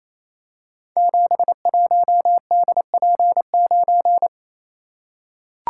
Morse code, heard as a keyed tone, about 28 wpm, 700 Hz. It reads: 71DP9
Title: Code Groups - 5 Letter:Numbers - 28wpm